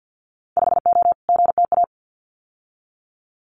Morse code, heard as a keyed tone, about 50 wpm, 700 Hz. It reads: 5O GTA
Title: Winter Field Day Exchanges Encore 50wpm